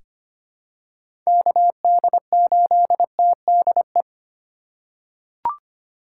Morse code, heard as a keyed tone, about 25 wpm, 700 Hz. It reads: KD8TDE
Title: USA Call Signs - 6 Characters - 2x Character Spacing - 25wpm